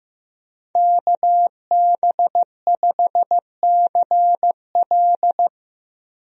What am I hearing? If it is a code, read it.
Morse code, heard as a keyed tone, about 15 wpm, 700 Hz. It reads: KB5CL
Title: USA Call Signs - 5 Characters - Repeated 3x - 15wpm